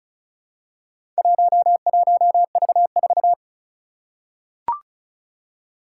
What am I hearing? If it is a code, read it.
Morse code, heard as a keyed tone, about 36 wpm, 700 Hz. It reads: 11V4
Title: Four Challenging Letter-Number Combinations 35wpm